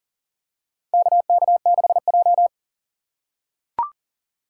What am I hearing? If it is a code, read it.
Morse code, heard as a keyed tone, about 40 wpm, 700 Hz. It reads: KK6J
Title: USA Call Signs - 4 Characters - 2x Character Spacing - 40wpm